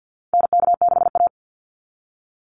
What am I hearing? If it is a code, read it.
Morse code, heard as a keyed tone, about 50 wpm, 700 Hz. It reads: NK6A
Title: NA SOTA Chaser Callsigns 50wpm